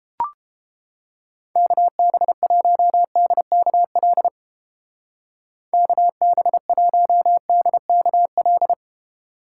Morse code, heard as a keyed tone, about 29 wpm, 700 Hz. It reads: KB1DKL KB1DKL